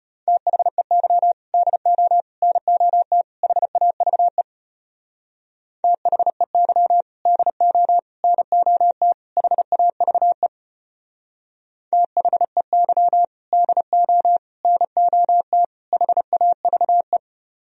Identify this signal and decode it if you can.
Morse code, heard as a keyed tone, about 34 wpm, 700 Hz. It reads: THEY DO NOT HAVE THEY DO NOT HAVE THEY DO NOT HAVE